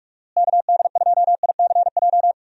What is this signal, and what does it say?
Morse code, heard as a keyed tone, about 42 wpm, 700 Hz. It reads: KD2ICJ
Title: USA Call Signs - 6 Characters - Repeated 3x - 45wpm